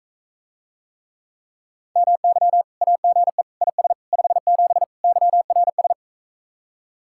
Morse code, heard as a keyed tone, about 46 wpm, 700 Hz. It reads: MY AGE IS 57 YRS